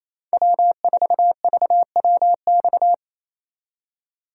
Morse code, heard as a keyed tone, about 37 wpm, 700 Hz. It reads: W4VWX